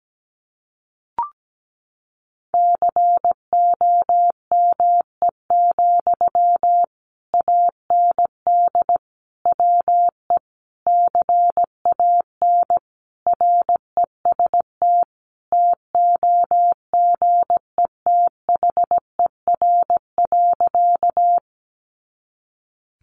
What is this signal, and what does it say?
Morse code, heard as a keyed tone, about 17 wpm, 700 Hz. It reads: COME, AND WE CAN REST TOGETHER.
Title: Sentences from Top 500 Words 17wpm